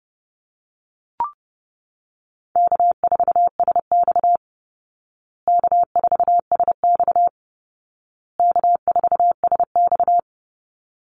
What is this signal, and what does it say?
Morse code, heard as a keyed tone, about 30 wpm, 700 Hz. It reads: K4SX K4SX K4SX
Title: Easy Call Signs Repeated 3x - 30wpm